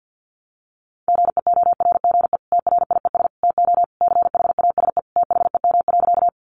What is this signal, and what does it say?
Morse code, heard as a keyed tone, about 50 wpm, 700 Hz. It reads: GEORGE TRIES TO CHASE THEM.
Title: Additional Easy Phrases 50wpm